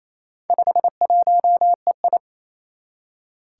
Morse code, heard as a keyed tone, about 28 wpm, 700 Hz. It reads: 51EI
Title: Four Challenging Letter-Number Combinations 28wpm